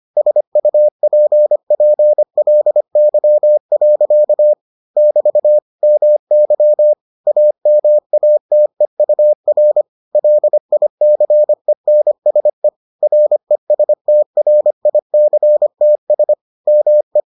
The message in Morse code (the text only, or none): SUPPLY. <BT> MY AMATEUR LICENSE RESTRICTS ME